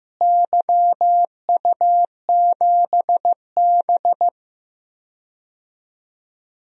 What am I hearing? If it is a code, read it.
Morse code, heard as a keyed tone, about 15 wpm, 700 Hz. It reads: YU7B
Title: DXCC-100 Call Signs - 4 Characters - Repeated 2x - 15wpm